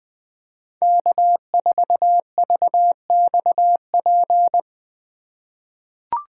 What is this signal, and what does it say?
Morse code, heard as a keyed tone, about 20 wpm, 700 Hz. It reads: K4VXP
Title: USA Call Signs - 5 Characters - 3x Character Spacing - 20wpm